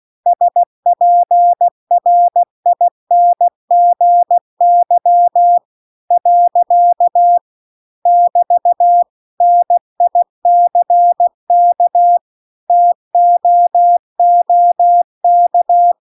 Morse code, heard as a keyed tone, about 16 wpm, 700 Hz. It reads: SPRINGY . <BT> NICK TOOK